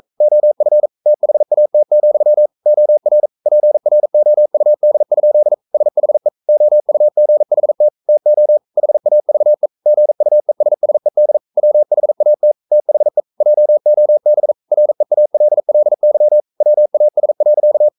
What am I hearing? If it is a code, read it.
Morse code, heard as a keyed tone, about 44 wpm, 600 Hz. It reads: OF THAT, OR PROUD? SHE OUGHT TO HAVE GUESSED WHAT THE JOB REALLY WAS.
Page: Romanen Never af Ken Follet på engelsk.